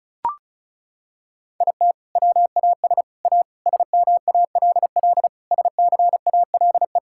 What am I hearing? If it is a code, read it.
Morse code, heard as a keyed tone, about 36 wpm, 700 Hz. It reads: IT WAS A SMALL SCALE
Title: Sentences from Top 1000 Words - Review Part 5 - 35wpm